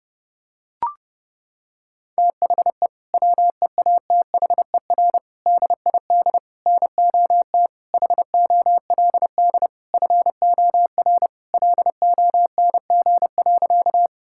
Morse code, heard as a keyed tone, about 32 wpm, 700 Hz. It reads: THE WEATHER DID NOT HOLD FOR LONG.
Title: Sentences from Top 1000 Words - Review Part 5 - 30wpm